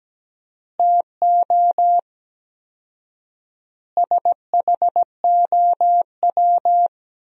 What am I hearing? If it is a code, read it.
Morse code, heard as a keyed tone, about 17 wpm, 700 Hz. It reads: TO SHOW
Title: 4x Word-Spacing - Sentences from Top 200 Words - 17wpm